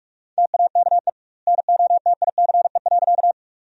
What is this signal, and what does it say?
Morse code, heard as a keyed tone, about 48 wpm, 700 Hz. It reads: TAKE NOTICE.
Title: 2x Word-Spacing - Sentences from Top 400 Words - 45wpm